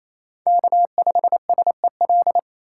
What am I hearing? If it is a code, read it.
Morse code, heard as a keyed tone, about 28 wpm, 700 Hz. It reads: K5SEL
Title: USA Call Signs - 5 Characters - 4x Character Spacing - 28wpm